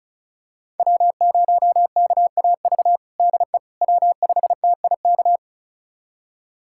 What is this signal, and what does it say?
Morse code, heard as a keyed tone, about 36 wpm, 700 Hz. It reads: W0KAV DE W5TIK
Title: USA Callsign DE Callsign - 5 Characters - 35wpm